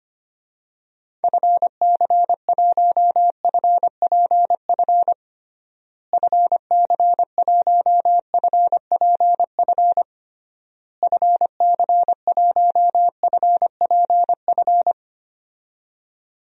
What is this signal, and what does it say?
Morse code, heard as a keyed tone, about 25 wpm, 700 Hz. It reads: FC1FPF FC1FPF FC1FPF
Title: DXCC-100 Call Signs - 6 Characters - Repeated 3x - 25wpm